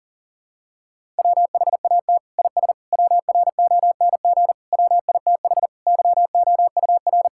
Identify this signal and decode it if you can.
Morse code, heard as a keyed tone, about 40 wpm, 700 Hz. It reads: WHAT IS WRONG WITH YOUR
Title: Sentences from Top 1000 Words - Review Part 5 - 40wpm